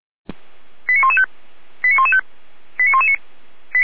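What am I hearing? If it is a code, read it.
Five-tone selective call (ZVEI), heard as quick runs of five tones, tones of about 70 ms. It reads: 89115 89115 89119
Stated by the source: ZVEI five-tone selective call received on SDR.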